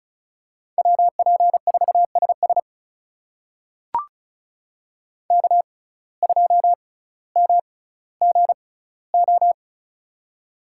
Morse code, heard as a keyed tone, about 36 wpm, spaced as for 18 wpm, 700 Hz. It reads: WP4SS K2MGO
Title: USA Call Signs - 5 Characters - 6x Character Spacing - 35wpm